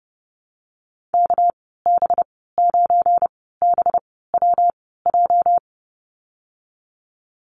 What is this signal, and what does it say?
Morse code, heard as a keyed tone, about 30 wpm, spaced as for 17 wpm, 700 Hz. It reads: KB9BWJ
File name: Lesson 37g - Letter x - Call Signs 30wpm